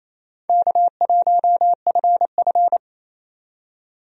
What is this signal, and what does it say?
Morse code, heard as a keyed tone, about 28 wpm, 700 Hz. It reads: K1FF
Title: USA Call Signs - 4 Characters - 28wpm